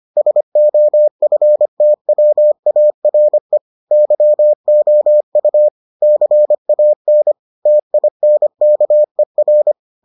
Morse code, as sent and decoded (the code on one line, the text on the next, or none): ... --- ..-. - .-- .- .-. / -.-- --- ..- / -.-. .- -. / - .. -. -.- . .-.
SOFTWARE YOU CAN TINKER